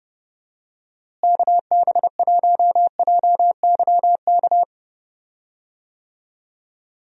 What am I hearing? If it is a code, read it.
Morse code, heard as a keyed tone, about 31 wpm, 700 Hz. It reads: KB1JYK
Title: USA Call Signs - 6 Characters - Repeated 2x - 30wpm